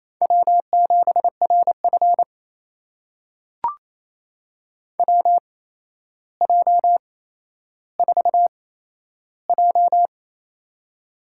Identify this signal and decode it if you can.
Morse code, heard as a keyed tone, about 28 wpm, spaced as for 12 wpm, 700 Hz. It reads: W7RF WJ4J
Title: USA Call Signs - 4 Characters - 8x Character Spacing - 28wpm